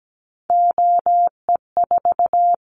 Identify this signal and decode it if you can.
Morse code, heard as a keyed tone, about 17 wpm, 700 Hz. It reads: OE4
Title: Callsign Prefixes for Top 100 DXCC - 17wpm